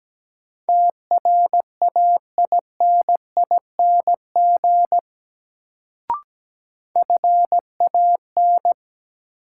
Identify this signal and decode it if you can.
Morse code, heard as a keyed tone, about 17 wpm, 700 Hz. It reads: TRAINING FAN
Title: Sentences from Top 1000 Words - Review Part 4 - New Word Drill - 17wpm